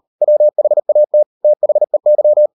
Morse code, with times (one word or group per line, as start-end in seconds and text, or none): 0.2-1.2 WHAT
1.4-2.5 THEY